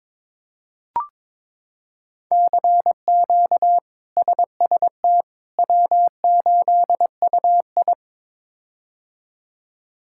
Morse code, heard as a keyed tone, about 22 wpm, 700 Hz. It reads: CQ SST W8UI
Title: K1USN SST QSO Practice - 22wpm